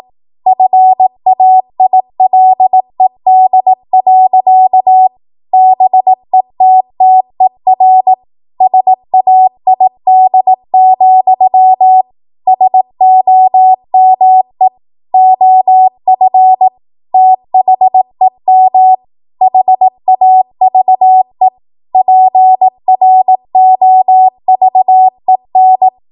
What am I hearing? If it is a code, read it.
Morse code, heard as a keyed tone, about 18 wpm, 750 Hz. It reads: FAILED. BETTER SAID, SOME OF THEM HAVE PROVEN
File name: Morse / 18WPM